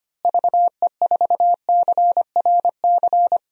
Morse code, heard as a keyed tone, about 25 wpm, 700 Hz. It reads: VE4CRC
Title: DXCC-100 Call Signs - 6 Characters - Repeated 2x - 25wpm